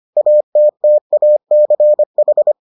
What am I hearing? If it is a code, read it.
Morse code, heard as a keyed tone, about 25 wpm, 600 Hz. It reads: ATTACH